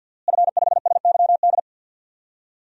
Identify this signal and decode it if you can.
Morse code, heard as a keyed tone, about 46 wpm, 700 Hz. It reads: F5SYD
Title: DXCC-100 Call Signs - 5 Characters - Repeated 2x - 50wpm